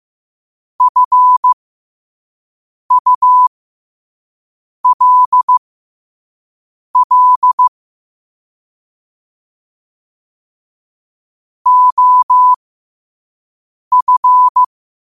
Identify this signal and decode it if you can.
Morse code, heard as a keyed tone, about 15 wpm, spaced as for 5 wpm, 1000 Hz. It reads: FULL OF
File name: qotdfiles / 5wpm